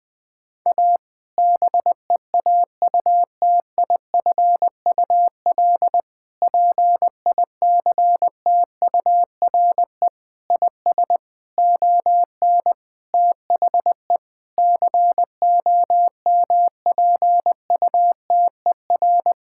Morse code, heard as a keyed tone, about 21 wpm, 700 Hz. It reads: A BEAUTIFUL PICTURE IS ON THE COMPUTER
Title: Sentences from Top 1000 Words - Review Part 6 - 20wpm